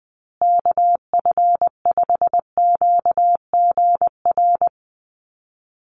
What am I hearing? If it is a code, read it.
Morse code, heard as a keyed tone, about 20 wpm, 700 Hz. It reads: KF5QGR
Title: USA Call Signs Repeated 3x 20wpm